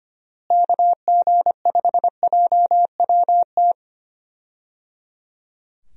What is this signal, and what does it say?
Morse code, heard as a keyed tone, about 25 wpm, 700 Hz. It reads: KG5JWT
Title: USA Call Signs - 6 Characters - Repeated 3x - 25wpm